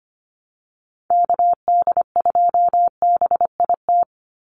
Morse code, heard as a keyed tone, about 25 wpm, 700 Hz. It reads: KD2BIT
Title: USA Call Signs 25wpm